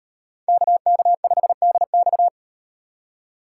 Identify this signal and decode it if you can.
Morse code, heard as a keyed tone, about 38 wpm, 700 Hz. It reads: KK5DX